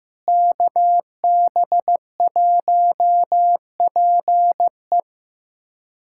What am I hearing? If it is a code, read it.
Morse code, heard as a keyed tone, about 16 wpm, 700 Hz. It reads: KB1PE
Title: USA Callsign DE Callsign 15wpm